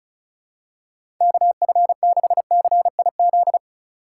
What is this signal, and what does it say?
Morse code, heard as a keyed tone, about 36 wpm, 700 Hz. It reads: KF6CIZ
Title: USA Call Signs - 6 Characters - 4x Character Spacing - 35wpm